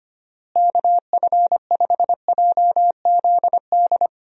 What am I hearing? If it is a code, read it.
Morse code, heard as a keyed tone, about 25 wpm, 700 Hz. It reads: KF5JZD
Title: USA Call Signs - 6 Characters - Repeated 2x - 25wpm